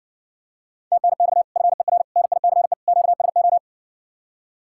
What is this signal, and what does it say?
Morse code, heard as a keyed tone, about 61 wpm, 700 Hz. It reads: TNX FER NICE QSO